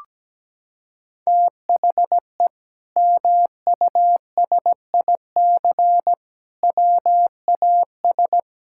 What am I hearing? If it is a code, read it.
Morse code, heard as a keyed tone, about 17 wpm, 700 Hz. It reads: THE MUSIC WAS
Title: Sentences from Top 1000 Words - Review Part 6 - 17wpm